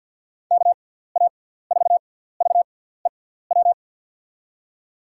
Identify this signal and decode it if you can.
Morse code, heard as a keyed tone, about 52 wpm, spaced as for 17 wpm, 700 Hz. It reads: KA4VEW
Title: USA Call Signs - 6 Characters - 6x Character Spacing - 50wpm